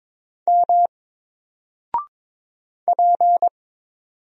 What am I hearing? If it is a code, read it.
Morse code, heard as a keyed tone, about 22 wpm, 700 Hz. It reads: M P
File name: Single-Letter-Number-22wpm